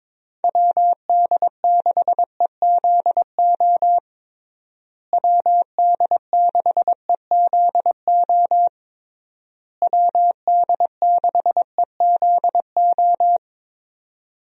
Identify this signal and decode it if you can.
Morse code, heard as a keyed tone, about 21 wpm, 700 Hz. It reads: WD6EZO WD6EZO WD6EZO
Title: USA Call Signs - 6 Characters - Repeated 3x - 22wpm